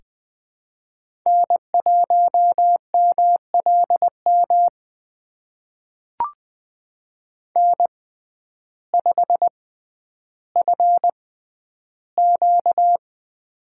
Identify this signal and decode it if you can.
Morse code, heard as a keyed tone, about 20 wpm, spaced as for 11 wpm, 700 Hz. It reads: N1MLM N5FQ
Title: USA Call Signs - 6x Character Spacing - 20wpm